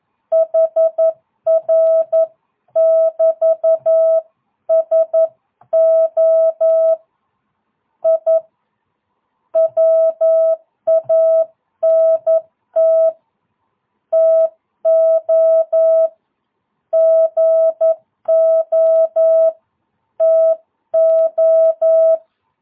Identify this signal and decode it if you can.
Morse code, heard as a keyed tone, about 11 wpm, 660 Hz. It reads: HR <BT> SO I WANT TO GO TO
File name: jh8cbh / 2022cw